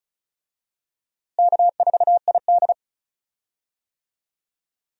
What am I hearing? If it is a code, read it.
Morse code, heard as a keyed tone, about 34 wpm, 700 Hz. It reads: K4ID
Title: USA Call Signs - 4 Characters - Repeated 2x - 35wpm